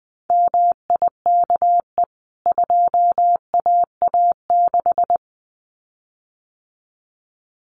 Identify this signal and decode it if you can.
Morse code, heard as a keyed tone, about 20 wpm, 700 Hz. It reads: MIKE 2AA6
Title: CWT Exchange 20wpm